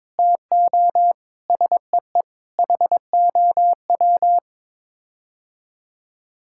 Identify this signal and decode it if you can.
Morse code, heard as a keyed tone, about 22 wpm, 700 Hz. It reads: TO SEE HOW
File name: practice / 22wpm / Sets of 7 Words - 22wpm